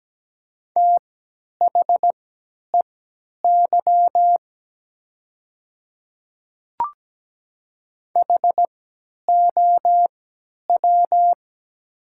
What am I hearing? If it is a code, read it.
Morse code, heard as a keyed tone, about 17 wpm, spaced as for 10 wpm, 700 Hz. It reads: THEY HOW
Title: 3x Character Spacing - Top 300 Words - 17wpm